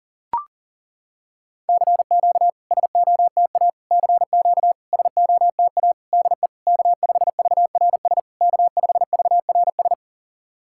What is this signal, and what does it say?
Morse code, heard as a keyed tone, about 40 wpm, 690 Hz. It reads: CQ SOTA CQ SOTA DE K5VRS K5VRS
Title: SOTA QSO Practice 40wpm